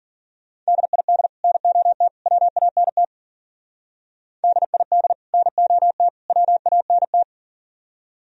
Morse code, heard as a keyed tone, about 48 wpm, 700 Hz. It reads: DID NOT WANT DID NOT WANT
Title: Speed-Racing - Sets of 3 Words 40wpm